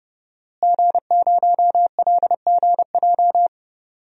Morse code, heard as a keyed tone, about 30 wpm, 700 Hz. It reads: G0LGJ
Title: DXCC-100 Call Signs - 5 Characters - Repeated 2x - 30wpm